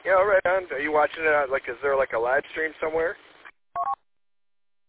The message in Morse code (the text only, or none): none